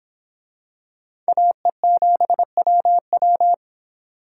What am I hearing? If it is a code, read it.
Morse code, heard as a keyed tone, about 26 wpm, 700 Hz. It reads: AE7WW